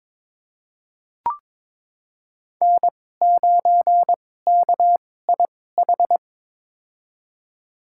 Morse code, heard as a keyed tone, about 22 wpm, spaced as for 16 wpm, 700 Hz. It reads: N9KIH
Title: USA Call Signs - 5 Characters - 2x Character Spacing - 22wpm